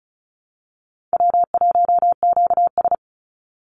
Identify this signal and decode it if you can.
Morse code, heard as a keyed tone, about 34 wpm, 700 Hz. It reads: W1QS
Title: Easy Call Signs Repeated 2x - 35wpm